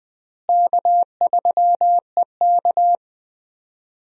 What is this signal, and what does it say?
Morse code, heard as a keyed tone, about 20 wpm, 700 Hz. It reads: K3EK
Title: USA Call Signs - 4 Characters - Repeated 3x - 20wpm